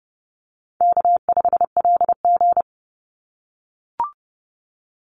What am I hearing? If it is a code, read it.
Morse code, heard as a keyed tone, about 32 wpm, 700 Hz. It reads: K5LG
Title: Call Signs - 30wpm